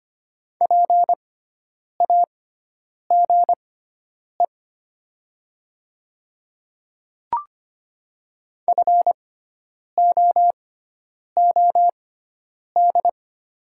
Morse code, heard as a keyed tone, about 25 wpm, spaced as for 9 wpm, 700 Hz. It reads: PAGE FOOD
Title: 6x Character Spacing - Top 300 Words - 25wpm